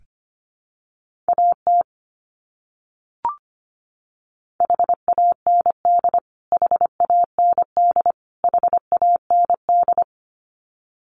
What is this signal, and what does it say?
Morse code, heard as a keyed tone, about 25 wpm, 700 Hz. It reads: AT HAND HAND HAND
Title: Top 300 Words Repeated 25wpm